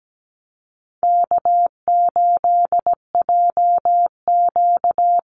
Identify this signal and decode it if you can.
Morse code, heard as a keyed tone, about 17 wpm, 700 Hz. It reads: K8JQ
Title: Easy Call Signs Repeated 3x - 17wpm